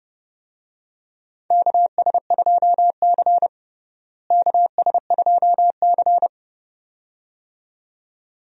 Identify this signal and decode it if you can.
Morse code, heard as a keyed tone, about 30 wpm, 700 Hz. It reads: KS2C KS2C
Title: USA Call Signs - 4 Characters - Repeated 2x - 30wpm